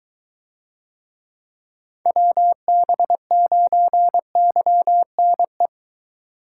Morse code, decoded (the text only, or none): WB9YNE